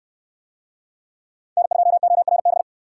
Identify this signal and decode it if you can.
Morse code, heard as a keyed tone, about 64 wpm, 700 Hz. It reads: N2ORB